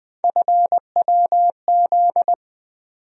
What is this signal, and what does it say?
Morse code, heard as a keyed tone, about 20 wpm, 700 Hz. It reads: FWZ
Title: USA Call Sign Postfixes - 3 Letters 20wpm